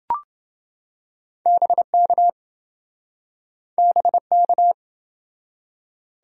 Morse code, heard as a keyed tone, about 27 wpm, 700 Hz. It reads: BK BK